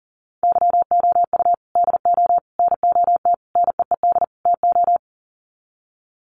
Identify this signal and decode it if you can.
Morse code, heard as a keyed tone, about 37 wpm, 700 Hz. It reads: YOU DO NOT NEED TO
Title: Sets of 5 Words 40wpm